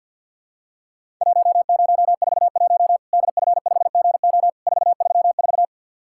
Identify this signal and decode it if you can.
Morse code, heard as a keyed tone, about 50 wpm, 700 Hz. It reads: 1041 DF5GO 434
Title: Worked All Europe DX Contest - QTC Exchange with Repeats - 50wpm